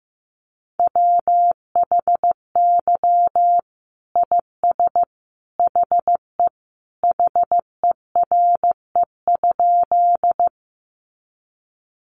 Morse code, heard as a keyed tone, about 15 wpm, 700 Hz. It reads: WHY IS HE HERE?
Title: Sentences from Top 200 Words 15wpm